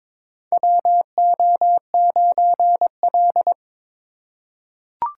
Morse code, heard as a keyed tone, about 22 wpm, 700 Hz. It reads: WO9L
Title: USA Call Signs - 4 Characters - 2x Character Spacing - 22wpm